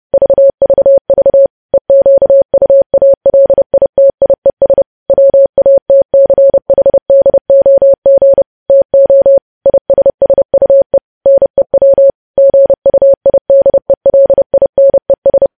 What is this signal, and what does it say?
Morse code, heard as a keyed tone, about 30 wpm, 550 Hz. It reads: VVV EQUALITIES WATCHDOG TO ISSUE NEW GUIDELINES
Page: news headlines in morse code at 30 wpm